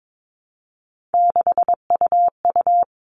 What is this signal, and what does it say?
Morse code, heard as a keyed tone, about 22 wpm, 700 Hz. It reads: 6UU
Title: All Cut Numbers - 3-Digit 22wpm